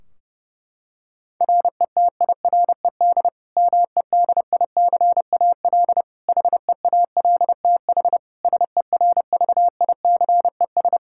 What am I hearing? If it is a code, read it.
Morse code, heard as a keyed tone, about 30 wpm, 700 Hz. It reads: RETIRED MEDICAL HEALTH SERVICES
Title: Practice QSO - Exchange 3 - More Info - 30wpm